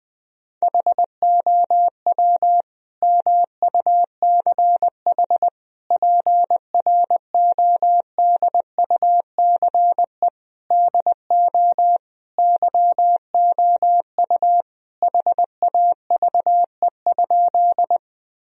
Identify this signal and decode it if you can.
Morse code, heard as a keyed tone, about 20 wpm, 700 Hz. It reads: HOW MUCH PRODUCE DO YOU HAVE?
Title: Sentences from Top 1000 Words - Review Part 3 - 20wpm